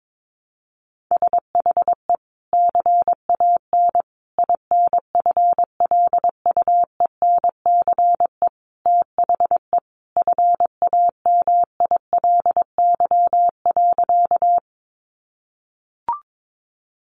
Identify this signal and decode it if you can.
Morse code, heard as a keyed tone, about 22 wpm, 700 Hz. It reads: SHE CAN INFLUENCE THE FAMILY.
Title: Sentences from Top 1100 Words 22wpm